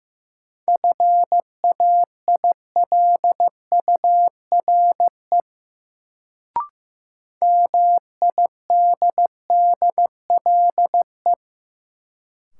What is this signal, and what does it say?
Morse code, heard as a keyed tone, about 15 wpm, 700 Hz. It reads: FAILURE MIDDLE
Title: Sentences from Top 900 Words - New Word Drill - 15wpm